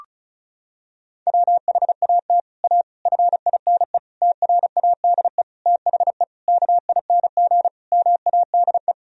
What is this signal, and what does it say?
Morse code, heard as a keyed tone, about 34 wpm, 700 Hz. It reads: WHAT A FINE TRADE THE KING MADE
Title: Sentences from Top 1000 Words - Review Part 4 - 35wpm